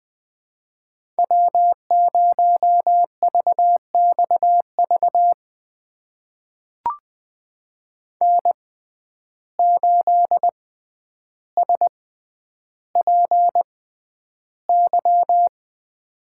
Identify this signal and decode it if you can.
Morse code, heard as a keyed tone, about 20 wpm, spaced as for 10 wpm, 700 Hz. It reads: W0VXV N8SPY
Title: USA Call Signs - 5 Characters - 6x Character Spacing - 20wpm